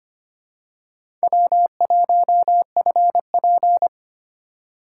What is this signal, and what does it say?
Morse code, heard as a keyed tone, about 25 wpm, 700 Hz. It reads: W1FP
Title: USA Call Signs - 4 Characters - 2x Character Spacing - 25wpm